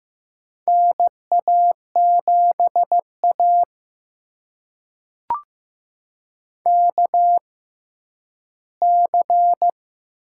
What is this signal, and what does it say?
Morse code, heard as a keyed tone, about 15 wpm, spaced as for 10 wpm, 700 Hz. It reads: NA7A KC
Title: USA Call Signs - 4 Characters - 6x Character Spacing - 15wpm